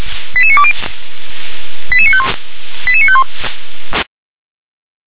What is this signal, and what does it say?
Five-tone selective call (ZVEI), heard as quick runs of five tones, tones of about 70 ms. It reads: 88920 88951 88951